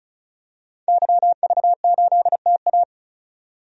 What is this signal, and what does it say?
Morse code, heard as a keyed tone, about 34 wpm, 700 Hz. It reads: YV8TA
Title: DXCC-100 Call Signs - 5 Characters - 35wpm